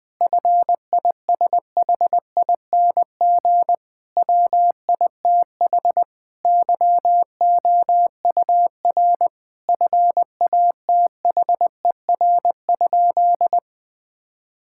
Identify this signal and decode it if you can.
Morse code, heard as a keyed tone, about 20 wpm, 700 Hz. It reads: FISHING WITH YOUR FATHER?
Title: Sentences from Top 1000 Words - Review Part 5 - 20wpm